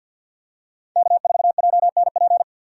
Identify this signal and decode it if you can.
Morse code, heard as a keyed tone, about 53 wpm, 700 Hz. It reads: K4JNP
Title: USA Call Signs - 5 Characters - 8x Character Spacing - 50wpm